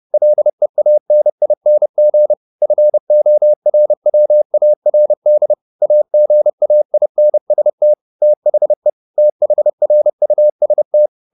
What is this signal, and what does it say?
Morse code, heard as a keyed tone, about 30 wpm, 600 Hz. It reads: LEANING FORWARD AGAINST THE THRUST
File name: the old man and the sea / the old man and the sea 1 30wpm